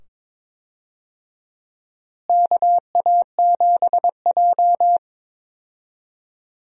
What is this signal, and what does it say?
Morse code, heard as a keyed tone, about 23 wpm, 700 Hz. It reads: KA7J
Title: USA Call Signs - 4 Characters - Repeated 2x - 22wpm